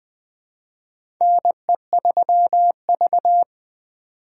Morse code, heard as a keyed tone, about 20 wpm, 700 Hz. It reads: NE3V
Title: USA Call Signs - 4 Characters - 3x Character Spacing - 20wpm